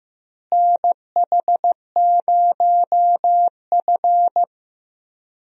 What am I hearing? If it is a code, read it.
Morse code, heard as a keyed tone, about 15 wpm, 700 Hz. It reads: NH0F
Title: USA Call Signs - 4 Characters - Repeated 2x - 15wpm